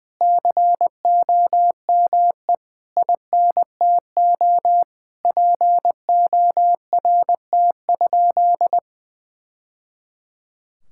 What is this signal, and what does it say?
Morse code, heard as a keyed tone, about 21 wpm, 700 Hz. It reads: COME INTO PORT?
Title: Sentences from Top 1000 Words - Review Part 5 - 20wpm